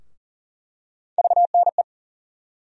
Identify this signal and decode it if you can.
Morse code, heard as a keyed tone, about 40 wpm, 700 Hz. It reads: VNE
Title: USA Call Sign Postfixes - 3 Letters 40wpm